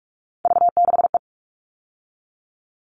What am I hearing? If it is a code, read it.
Morse code, heard as a keyed tone, about 44 wpm, 700 Hz. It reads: V6E
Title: All Cut Numbers - 3-Digit 45wpm